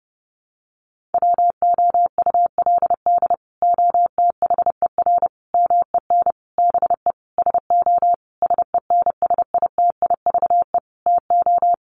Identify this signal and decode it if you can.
Morse code, heard as a keyed tone, about 30 wpm, 700 Hz. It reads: WOULD OTHER MEN BE SO SENSITIVE TO
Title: Sentences from Top 1400 Words 30wpm